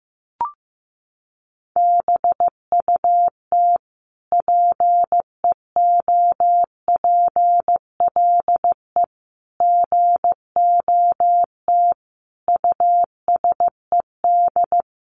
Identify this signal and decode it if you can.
Morse code, heard as a keyed tone, about 15 wpm, 700 Hz. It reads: BUT PEOPLE GOT USED
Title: Sentences from Top 700 Words 15wpm